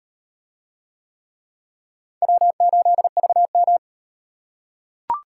Morse code, heard as a keyed tone, about 34 wpm, 700 Hz. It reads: W8VM